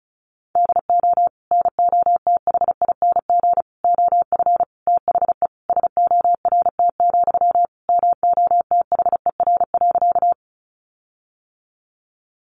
Morse code, heard as a keyed tone, about 34 wpm, 700 Hz. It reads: DO NOTHING OF THE SORT, MOTHER.
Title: Sentences from Top 1300 Words 35wpm